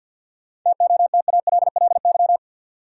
Morse code, heard as a keyed tone, about 52 wpm, 700 Hz. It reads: TOTALLY
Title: Sentences from Top 1600 Words - New Word Drill Alt - 50wpm